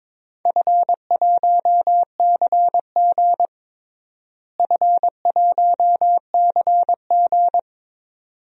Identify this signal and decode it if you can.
Morse code, heard as a keyed tone, about 21 wpm, 700 Hz. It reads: F1CG F1CG
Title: DXCC-100 Call Signs - 4 Characters - Repeated 2x - 22wpm